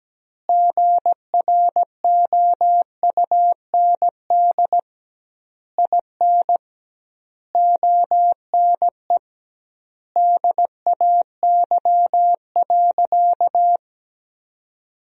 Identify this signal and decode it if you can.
Morse code, heard as a keyed tone, about 17 wpm, 700 Hz. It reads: GROUND IN ONE DAY.
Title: 2x Word-Spacing - Sentences from Top 500 Words - 17wpm